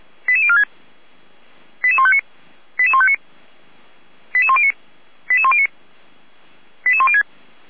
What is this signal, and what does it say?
Five-tone selective call (ZVEI), heard as quick runs of five tones, tones of about 70 ms. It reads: 88046 88159 88159 88109 88109 88106